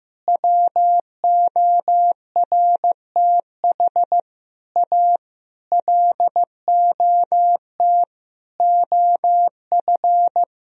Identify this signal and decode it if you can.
Morse code, heard as a keyed tone, about 15 wpm, 700 Hz. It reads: WORTH A LOT OF
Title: Sentences from Top 1000 Words - Review Part 4 - 15wpm